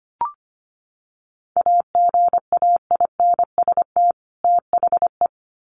Morse code, heard as a keyed tone, about 25 wpm, 700 Hz. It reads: AGAINST THE
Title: Sets of 2 Words 25wpm